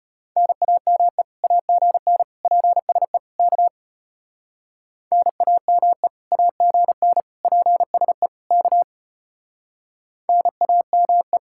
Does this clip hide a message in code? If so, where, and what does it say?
Yes: Morse code from 0.4 to 11.4 s, heard as a keyed tone, about 32 wpm, 700 Hz. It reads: NAME AGN PSE K NAME AGN PSE K NAME